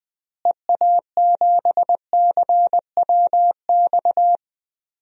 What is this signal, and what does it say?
Morse code, heard as a keyed tone, about 20 wpm, 700 Hz. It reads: EA7CWX
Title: DXCC-100 Call Signs - 6 Characters - Repeated 2x - 20wpm